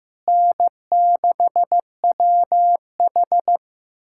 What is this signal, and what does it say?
Morse code, heard as a keyed tone, about 15 wpm, 700 Hz. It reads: N6WH
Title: USA Callsign DE Callsign - 4 Characters - 15wpm